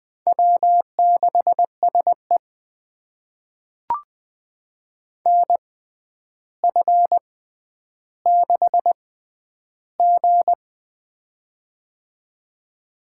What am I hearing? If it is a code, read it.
Morse code, heard as a keyed tone, about 20 wpm, spaced as for 10 wpm, 700 Hz. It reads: W6SE NF6G
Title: USA Call Signs - 4 Characters - 6x Character Spacing - 20wpm